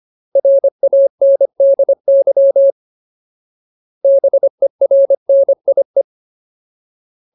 Morse code, heard as a keyed tone, about 25 wpm, 550 Hz. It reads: RANDY BERNIE
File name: Ultimate Word List / 25WPM Ultimate